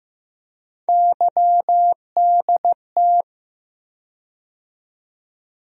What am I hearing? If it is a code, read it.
Morse code, heard as a keyed tone, about 15 wpm, 700 Hz. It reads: YDT
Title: USA Call Sign Postfixes - 3 Letters 15wpm